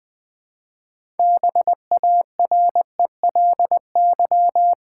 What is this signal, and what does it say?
Morse code, heard as a keyed tone, about 20 wpm, 700 Hz. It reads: BARELY
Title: Build A Word - 3 Letters At A Time - 6-Letter Words - 20wpm